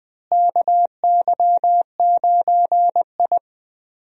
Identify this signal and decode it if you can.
Morse code, heard as a keyed tone, about 20 wpm, 700 Hz. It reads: KY9I
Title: USA Call Signs - 4 Characters - Repeated 3x - 20wpm